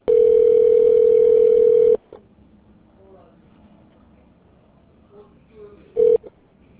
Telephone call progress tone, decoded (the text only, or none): ringback tone